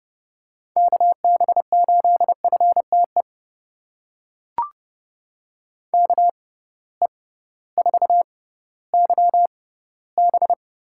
Morse code, heard as a keyed tone, about 30 wpm, spaced as for 16 wpm, 700 Hz. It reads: KB8FTE KE4YB
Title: USA Call Signs - 6 Characters - 6x Character Spacing - 30wpm